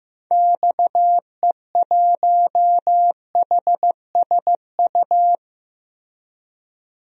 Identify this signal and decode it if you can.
Morse code, heard as a keyed tone, about 16 wpm, 700 Hz. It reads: XE1HSU
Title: DXCC-100 Call Signs - 6 Characters - Repeated 3x - 15wpm